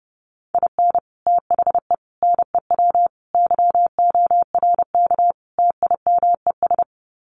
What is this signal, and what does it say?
Morse code, heard as a keyed tone, about 30 wpm, 700 Hz. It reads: IN THE NEW YORK TIMES
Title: Sets of 5 Words 30wpm